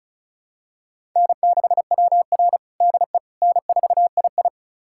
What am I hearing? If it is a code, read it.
Morse code, heard as a keyed tone, about 36 wpm, 700 Hz. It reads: N6WR DE N4II
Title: USA Callsign DE Callsign - 4 Characters - 35wpm